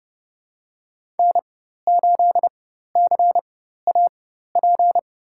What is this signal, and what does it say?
Morse code, heard as a keyed tone, about 31 wpm, spaced as for 14 wpm, 700 Hz. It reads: N8CAP
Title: USA Call Signs - 4x Character Spacing - 30wpm